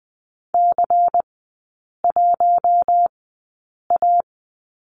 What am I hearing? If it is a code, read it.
Morse code, heard as a keyed tone, about 20 wpm, spaced as for 8 wpm, 700 Hz. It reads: C1A
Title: 093 Lesson 16e - Letter c - Three Characters 20wpm